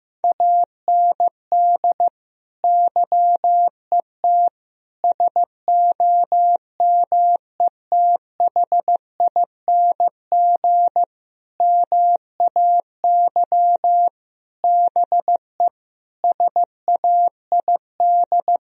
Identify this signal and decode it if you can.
Morse code, heard as a keyed tone, about 15 wpm, 700 Hz. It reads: AND YET SOMETHING MAY BE SAID
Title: Sentences from Top 1000 Words - Review Part 3 - 15wpm